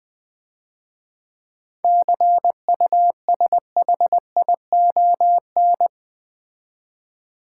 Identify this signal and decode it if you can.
Morse code, heard as a keyed tone, about 20 wpm, 700 Hz. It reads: CUSHION
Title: Easy Words - Speed-Racing with Hints - 20wpm